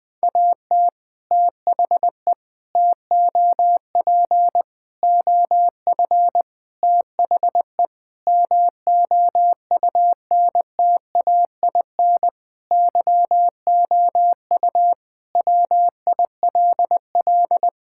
Morse code, heard as a keyed tone, about 20 wpm, 700 Hz. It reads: AT THE TOP OF THE MOUNTAIN YOU WILL
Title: Sentences with 3-Word Phrases - 20wpm